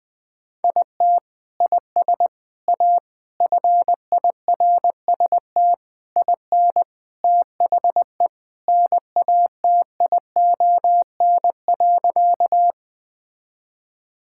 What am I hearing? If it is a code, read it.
Morse code, heard as a keyed tone, about 21 wpm, 700 Hz. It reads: IT IS A FIRST IN THE NATION.
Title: Sentences from Top 1000 Words - Review Part 6 - 20wpm